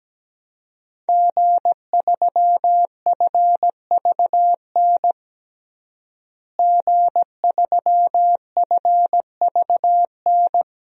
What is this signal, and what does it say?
Morse code, heard as a keyed tone, about 17 wpm, 700 Hz. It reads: G3FVN G3FVN
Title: DXCC-100 Call Signs - 5 Characters - Repeated 3x - 17wpm